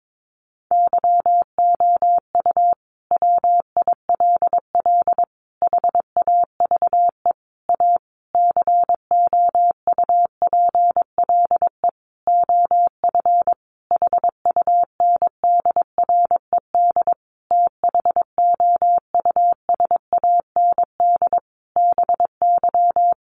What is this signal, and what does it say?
Morse code, heard as a keyed tone, about 22 wpm, 700 Hz. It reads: YOU WILL HAVE A COUPLE OF HUNDRED THOUSAND BY
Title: Sentences from Top 1100 Words 22wpm